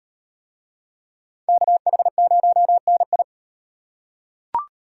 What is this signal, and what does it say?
Morse code, heard as a keyed tone, about 36 wpm, 700 Hz. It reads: KH0NI